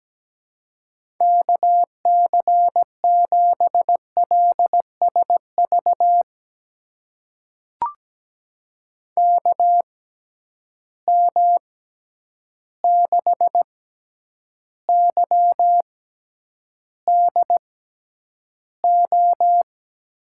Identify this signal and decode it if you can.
Morse code, heard as a keyed tone, about 17 wpm, spaced as for 9 wpm, 700 Hz. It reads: KC7LSV KM6YDO
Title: USA Call Signs - 6 Characters - 6x Character Spacing - 17wpm